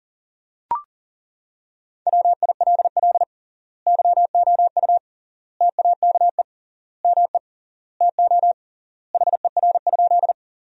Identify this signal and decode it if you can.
Morse code, heard as a keyed tone, about 37 wpm, 700 Hz. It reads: WILL YOU TAKE ME TO HER?
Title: 3x Word-Spacing - Sentences from Top 200 Words - 40wpm